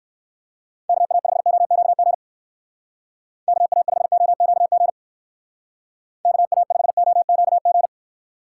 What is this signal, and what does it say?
Morse code, heard as a keyed tone, about 57 wpm, 700 Hz. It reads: KA5OQG KA5OQG KA5OQG